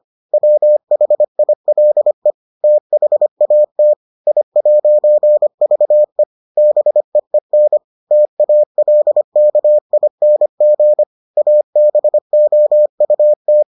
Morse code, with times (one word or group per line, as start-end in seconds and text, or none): 0.3-2.3 WHILE
2.6-3.9 THAT
4.3-6.2 I'VE
6.6-7.8 BEEN
8.1-11.0 TALKING
11.4-13.6 ABOUT